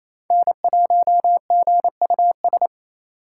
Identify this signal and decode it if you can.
Morse code, heard as a keyed tone, about 36 wpm, 700 Hz. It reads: N1GUS